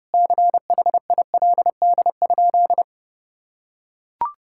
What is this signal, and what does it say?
Morse code, heard as a keyed tone, about 30 wpm, 700 Hz. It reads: CHILD?
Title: Sentences from Top 1600 Words 30wpm